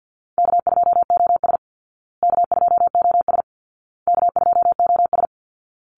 Morse code, heard as a keyed tone, about 50 wpm, 700 Hz. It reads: K2OS K2OS K2OS
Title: Easy Call Signs Repeated 3x - 50wpm